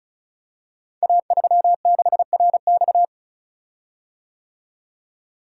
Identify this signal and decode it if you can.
Morse code, heard as a keyed tone, about 36 wpm, 700 Hz. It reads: A36RX
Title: Code Groups - 5 Letter:Numbers - 35wpm